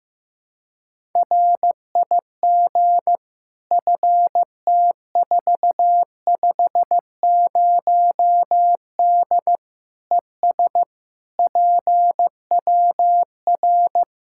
Morse code, heard as a keyed tone, about 16 wpm, 700 Hz. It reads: RIG FT450D ES PWR
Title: Practice QSO - Exchange 2 - Info - 15wpm